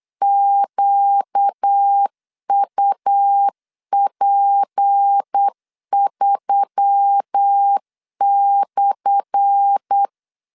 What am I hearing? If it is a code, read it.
Morse code, heard as a keyed tone, about 8 wpm, 800 Hz. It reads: QUP3/
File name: LEZIONI / 10 SIGNS / 10A SLASH 8wpm 3452